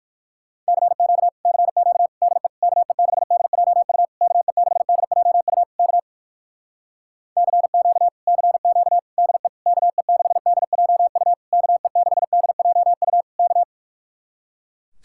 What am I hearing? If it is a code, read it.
Morse code, heard as a keyed tone, about 57 wpm, 700 Hz. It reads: CQ CQ DE KE6DJU KE6DJU K CQ CQ DE KE6DJU KE6DJU K